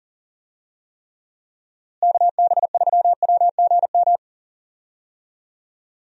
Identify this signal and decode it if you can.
Morse code, heard as a keyed tone, about 40 wpm, 700 Hz. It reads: KB3WGM
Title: USA Call Signs - 6 Characters - Repeated 3x - 40wpm